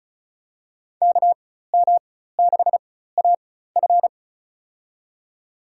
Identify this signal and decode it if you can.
Morse code, heard as a keyed tone, about 36 wpm, spaced as for 16 wpm, 700 Hz. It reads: KM6AF
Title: USA Call Signs - 5 Characters - 4x Character Spacing - 35wpm